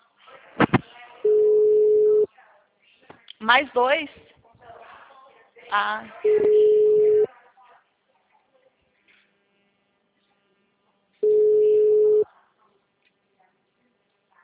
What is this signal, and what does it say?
Ringback tone (Europe, 425 Hz), with speech.